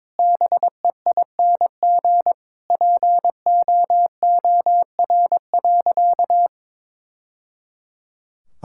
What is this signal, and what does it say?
Morse code, heard as a keyed tone, about 22 wpm, 700 Hz. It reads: BEING POOR.
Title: Sentences from Top 1000 Words - Review Part 6 - 22wpm